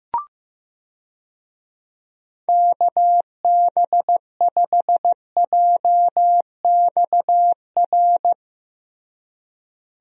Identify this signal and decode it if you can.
Morse code, heard as a keyed tone, about 15 wpm, 700 Hz. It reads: KB5JXR
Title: USA Call Signs - 6 Characters - Repeated 3x - 15wpm